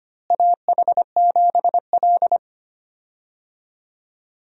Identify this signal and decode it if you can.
Morse code, heard as a keyed tone, about 25 wpm, 700 Hz. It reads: AH7L
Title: USA Call Signs - 4 Characters - Repeated 3x - 25wpm